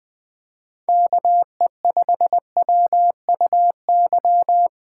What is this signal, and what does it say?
Morse code, heard as a keyed tone, about 20 wpm, 700 Hz. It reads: KE5WUY
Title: Speed Racing - USA 2x3 Call Signs 17wpm